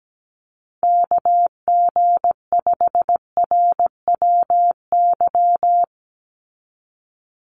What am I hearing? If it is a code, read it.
Morse code, heard as a keyed tone, about 17 wpm, 700 Hz. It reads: KG5RWY
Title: USA Call Signs 17wpm